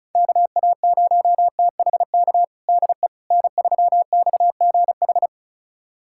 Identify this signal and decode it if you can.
Morse code, heard as a keyed tone, about 34 wpm, 700 Hz. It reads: KA0THK DE N3XGH
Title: USA Callsign DE Callsign 35wpm